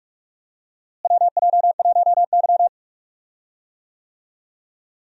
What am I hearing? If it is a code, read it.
Morse code, heard as a keyed tone, about 43 wpm, 700 Hz. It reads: WJ1Y
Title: USA Call Signs - 4 Characters - Repeated 3x - 45wpm